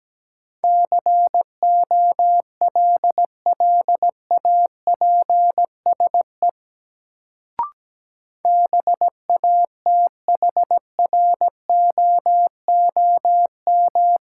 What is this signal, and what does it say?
Morse code, heard as a keyed tone, about 17 wpm, 700 Hz. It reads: COLLAPSE BATHROOM
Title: Sets of 5 Words Encore - New Word Drill - 17wpm